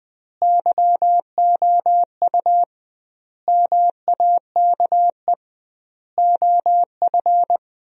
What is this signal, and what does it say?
Morse code, heard as a keyed tone, about 21 wpm, 700 Hz. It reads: YOU MAKE OF
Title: 2x Word-Spacing - Sentences from Top 100 Words - 20wpm